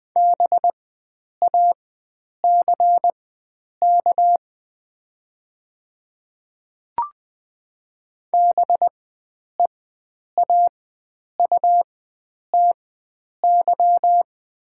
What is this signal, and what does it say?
Morse code, heard as a keyed tone, about 20 wpm, spaced as for 9 wpm, 700 Hz. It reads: BACK BEAUTY
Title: 4x Character Spacing - Top 500 Words - 20wpm